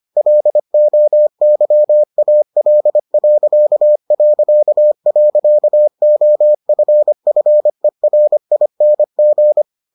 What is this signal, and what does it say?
Morse code, heard as a keyed tone, about 25 wpm, 600 Hz. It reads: LOYAL...OFFERING